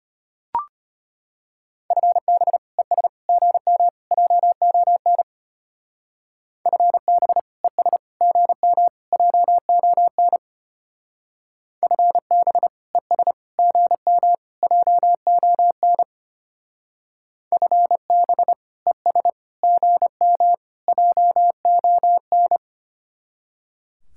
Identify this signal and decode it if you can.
Morse code, heard as a keyed tone, about 29 wpm, 700 Hz. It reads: FB ES GM JON FB ES GM JON FB ES GM JON FB ES GM JON
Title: Speed-Racing Common QSO Phrases - 25wpm